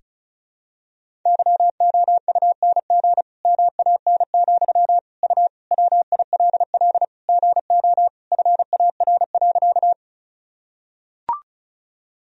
Morse code, heard as a keyed tone, about 36 wpm, 700 Hz. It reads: YOUNG MAN, U WILL GO FAR.
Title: Sentences from Top 400 Words and Abbreviations 35wpm